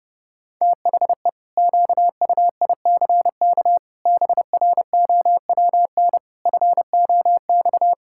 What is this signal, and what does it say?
Morse code, heard as a keyed tone, about 30 wpm, 700 Hz. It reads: THE QUICK BROWN FOX
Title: Pangram Challenge v2 - 30wpm